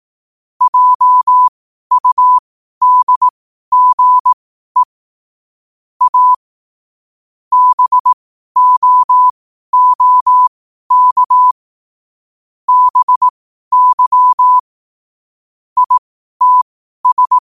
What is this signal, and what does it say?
Morse code, heard as a keyed tone, about 18 wpm, spaced as for 12 wpm, 1000 Hz. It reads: JUDGE A BOOK BY ITS
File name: qotdfiles / 12wpm